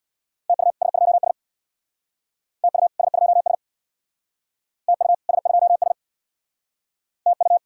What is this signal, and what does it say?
Morse code, heard as a keyed tone, about 62 wpm, 700 Hz. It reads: TU S2S TU S2S TU S2S TU
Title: Speed-Racing Common QSO Phrases - 50wpm